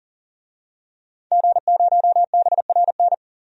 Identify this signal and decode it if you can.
Morse code, heard as a keyed tone, about 40 wpm, 700 Hz. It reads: G0BRN
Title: DXCC-100 Call Signs - 5 Characters - 40wpm